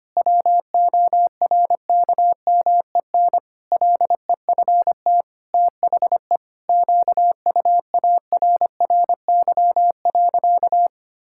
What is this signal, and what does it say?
Morse code, heard as a keyed tone, about 25 wpm, 700 Hz. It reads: WORKMEN LEFT THE QUARRY.
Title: Pangram Challenge v2 - 25wpm